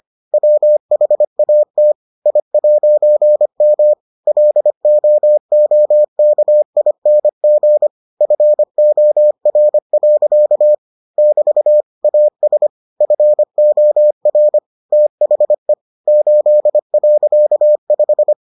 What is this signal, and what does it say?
Morse code, heard as a keyed tone, about 25 wpm, 600 Hz. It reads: WHAT I'M LOOKING FOR. <BT> AS FOR THE 8.5